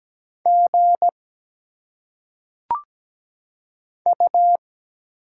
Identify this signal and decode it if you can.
Morse code, heard as a keyed tone, about 17 wpm, 700 Hz. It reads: G U
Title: Single Letters - Phonetic Pronunciation 17wpm